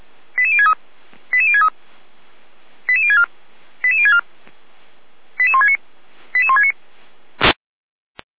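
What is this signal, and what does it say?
Five-tone selective call (ZVEI), heard as quick runs of five tones, tones of about 70 ms. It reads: 88063 88063 88064 88064 88169 88169
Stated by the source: ZVEI five-tone selective call received on SDR.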